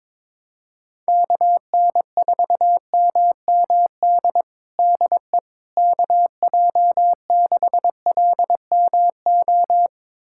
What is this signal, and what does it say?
Morse code, heard as a keyed tone, about 21 wpm, 700 Hz. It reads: KN4MMD DE KJ6LMO
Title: USA Callsign DE Callsign - 6 Characters - 22wpm